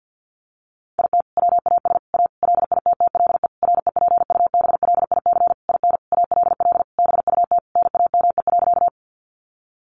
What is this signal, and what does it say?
Morse code, heard as a keyed tone, about 50 wpm, 700 Hz. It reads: IT WAS A LITTLE REPUBLIC IN ALL BUT NAME.
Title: Sentences from Top 1500 Words 50wpm